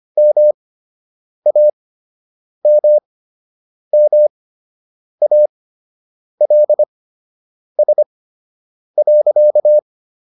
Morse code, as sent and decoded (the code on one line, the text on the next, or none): -- .- -- -- .- .-.. ... .-.-.-
MAMMALS.